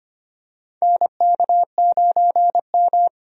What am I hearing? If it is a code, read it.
Morse code, heard as a keyed tone, about 25 wpm, 700 Hz. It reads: NK9M
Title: USA Call Signs - 4 Characters - 25wpm